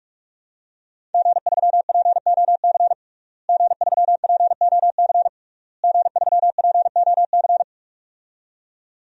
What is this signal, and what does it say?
Morse code, heard as a keyed tone, about 46 wpm, 700 Hz. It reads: G3POC G3POC G3POC
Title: DXCC-100 Call Signs - 5 Characters - Repeated 3x - 45wpm